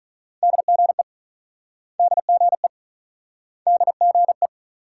Morse code, heard as a keyed tone, about 44 wpm, 700 Hz. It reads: DGE DGE DGE